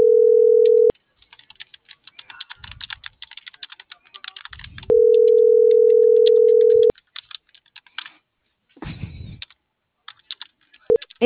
Ringback tone (North America), 440 + 480 Hz.